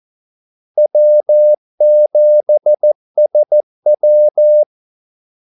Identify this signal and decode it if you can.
Morse code, heard as a keyed tone, about 14 wpm, 600 Hz. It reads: W7SW
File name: Callsigns-14wpm